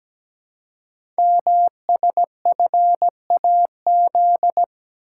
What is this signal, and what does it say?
Morse code, heard as a keyed tone, about 17 wpm, 700 Hz. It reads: MSFAZ
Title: Code Groups - 5 Letters - 17wpm